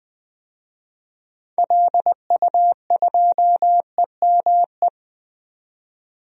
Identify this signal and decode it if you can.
Morse code, heard as a keyed tone, about 20 wpm, 700 Hz. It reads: LU2EME
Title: DXCC-100 Call Signs - 6 Characters - 20wpm